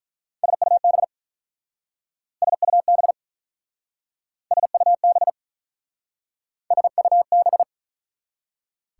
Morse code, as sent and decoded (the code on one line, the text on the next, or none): ... ..- -... / ... ..- -... / ... ..- -... / ... ..- -...
SUB SUB SUB SUB